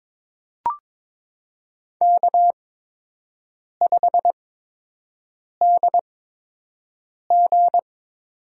Morse code, heard as a keyed tone, about 23 wpm, spaced as for 6 wpm, 700 Hz. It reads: K5DG
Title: USA Call Signs - 4 Characters - 8x Character Spacing - 22wpm